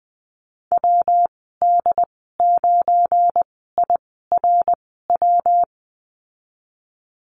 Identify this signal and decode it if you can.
Morse code, heard as a keyed tone, about 19 wpm, spaced as for 14 wpm, 700 Hz. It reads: WD9IRW
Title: 231 Lesson 37g - Letter x - Call Signs 20wpm